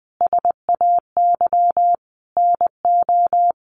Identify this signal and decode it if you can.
Morse code, heard as a keyed tone, about 20 wpm, 700 Hz. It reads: SAY NO
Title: Sentences from Top 1400 Words 20wpm